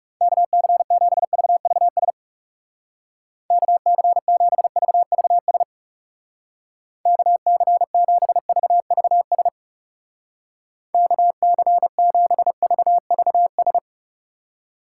Morse code, heard as a keyed tone, about 38 wpm, 700 Hz. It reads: KC7VVS KC7VVS KC7VVS KC7VVS